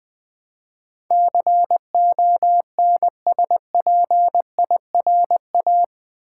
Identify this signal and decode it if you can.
Morse code, heard as a keyed tone, about 20 wpm, 700 Hz. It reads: CONSPIRA
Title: Build A Word - 4 Letters At A Time - 10-Letter Words - 20wpm